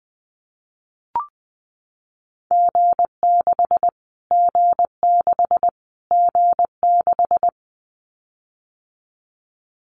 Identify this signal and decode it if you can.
Morse code, heard as a keyed tone, about 20 wpm, 700 Hz. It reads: G6 G6 G6
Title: Callsign Prefixes for Top 25 DXCC - Repeated 3x - 20wpm